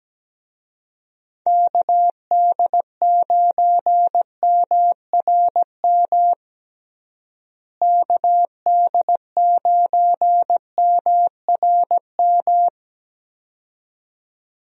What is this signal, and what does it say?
Morse code, heard as a keyed tone, about 17 wpm, 700 Hz. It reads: KD9MRM KD9MRM
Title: USA Call Signs - 6 Characters - Repeated 2x - 17wpm